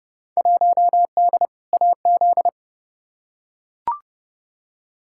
Morse code, heard as a keyed tone, about 29 wpm, 700 Hz. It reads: 1D AZ
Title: Field Day Exchanges - 30wpm